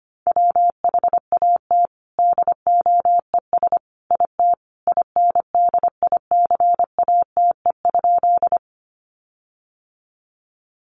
Morse code, heard as a keyed tone, about 25 wpm, 700 Hz. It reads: WHAT DOES IT INDICATE?
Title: Sentences from Top 1300 Words 25wpm